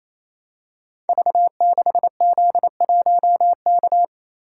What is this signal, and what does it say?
Morse code, heard as a keyed tone, about 27 wpm, 700 Hz. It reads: V6Z1K
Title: Code Groups - 5 Letter:Numbers - 28wpm